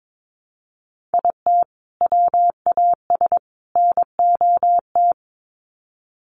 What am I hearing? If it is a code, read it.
Morse code, heard as a keyed tone, about 22 wpm, 700 Hz. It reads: IT WAS NOT
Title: Sets of 3 Words 22wpm